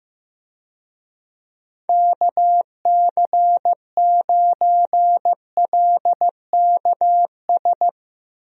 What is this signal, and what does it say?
Morse code, heard as a keyed tone, about 15 wpm, 700 Hz. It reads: KC9LKS
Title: USA Call Signs - 6 Characters - Repeated 3x - 15wpm